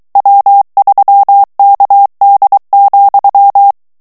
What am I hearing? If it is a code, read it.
Morse code, heard as a keyed tone, about 23 wpm, 800 Hz. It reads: W3KD,
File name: ANGLAIS texte clair 3 vitesse 22 mots minute